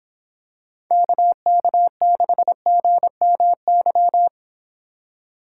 Morse code, heard as a keyed tone, about 26 wpm, 700 Hz. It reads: KK6GMY